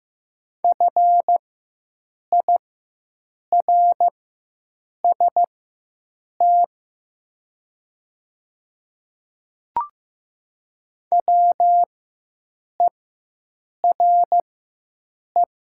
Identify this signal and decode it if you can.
Morse code, heard as a keyed tone, about 15 wpm, spaced as for 7 wpm, 700 Hz. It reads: FIRST WERE
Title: 4x Character Spacing - Top 500 Words - 15wpm